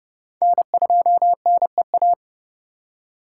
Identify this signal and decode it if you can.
Morse code, heard as a keyed tone, about 40 wpm, 700 Hz. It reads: N2NEA